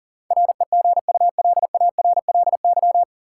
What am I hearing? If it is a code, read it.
Morse code, heard as a keyed tone, about 40 wpm, 700 Hz. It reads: REGULARLY
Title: Sentences from Top 1400 Words - New Word Drill Alt - 40wpm